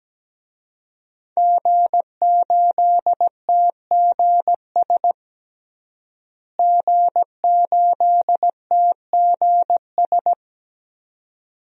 Morse code, heard as a keyed tone, about 17 wpm, 700 Hz. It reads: G8TGS G8TGS
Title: DXCC-100 Call Signs - 5 Characters - Repeated 2x - 17wpm